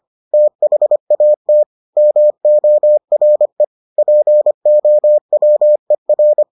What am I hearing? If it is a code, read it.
Morse code, heard as a keyed tone, about 25 wpm, 600 Hz. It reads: THAT MORE POWER